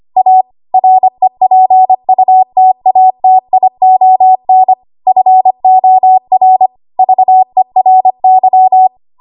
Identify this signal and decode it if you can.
Morse code, heard as a keyed tone, about 25 wpm, 750 Hz. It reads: A REPUTATION FOR VERY
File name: Archive / 25 WPM / 25WPM